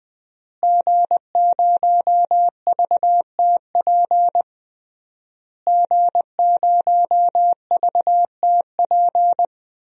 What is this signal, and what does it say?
Morse code, heard as a keyed tone, about 19 wpm, 700 Hz. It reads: G0VTP G0VTP
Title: DXCC-100 Call Signs - 5 Characters - Repeated 3x - 20wpm